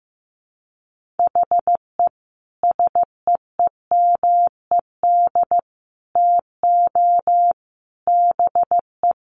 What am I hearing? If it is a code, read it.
Morse code, heard as a keyed tone, about 15 wpm, 700 Hz. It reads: HE SEEMED TO BE
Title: A Sound Of Thunder 15wpm